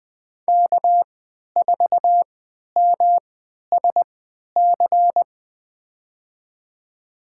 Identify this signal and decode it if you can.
Morse code, heard as a keyed tone, about 20 wpm, spaced as for 11 wpm, 700 Hz. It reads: K4MSC
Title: USA Call Signs - 5 Characters - 3x Character Spacing - 20wpm